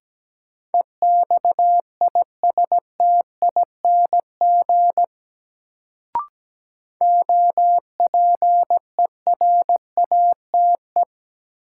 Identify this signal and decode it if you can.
Morse code, heard as a keyed tone, about 17 wpm, 700 Hz. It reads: EXISTING OPERATE
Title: Sentences from Top 1500 Words - New Word Drill Alt - 17wpm